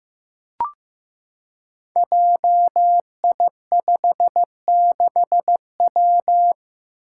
Morse code, heard as a keyed tone, about 15 wpm, 700 Hz. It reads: JI56W
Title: Five Challenging Letter-Number Combinations 15wpm